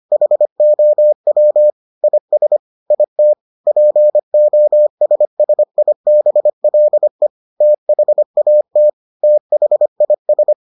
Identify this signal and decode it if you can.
Morse code, heard as a keyed tone, about 25 wpm, 600 Hz. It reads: HOW IS IT POSSIBLE THAT THIS